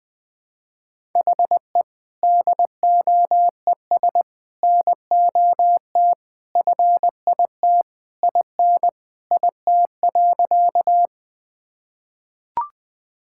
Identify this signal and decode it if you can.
Morse code, heard as a keyed tone, about 20 wpm, 700 Hz. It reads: HE DOES NOT FIT IN IT.
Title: Sentences from Top 1000 Words - Review Part 5 - 20wpm